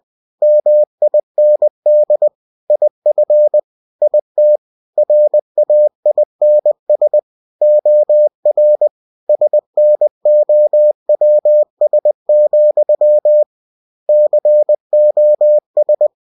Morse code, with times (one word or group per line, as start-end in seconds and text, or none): 0.4-2.3 MIND
2.7-3.6 IF
4.0-4.6 IT
5.0-7.2 RAINS
7.6-8.9 OR
9.3-13.4 SNOWS,
14.1-16.1 COS